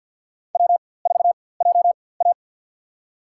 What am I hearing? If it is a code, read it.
Morse code, heard as a keyed tone, about 50 wpm, spaced as for 23 wpm, 700 Hz. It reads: W4JA
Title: USA Call Signs - 4x Character Spacing - 50wpm